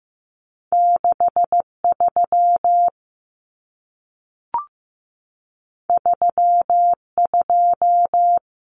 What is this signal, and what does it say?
Morse code, heard as a keyed tone, about 15 wpm, 700 Hz. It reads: 63 32
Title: Common Cut Numbers - 2-Digit 15wpm